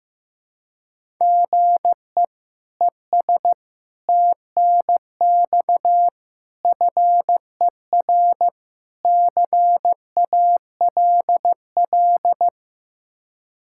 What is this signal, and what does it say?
Morse code, heard as a keyed tone, about 15 wpm, 700 Hz. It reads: GE ES TNX FER CALL
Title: K1USN SST QSO Practice 15wpm